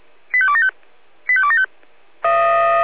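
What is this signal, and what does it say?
Five-tone selective call (ZVEI), heard as quick runs of five tones, tones of about 70 ms. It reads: 85275 85275
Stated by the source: ZVEI five-tone call received by SDR.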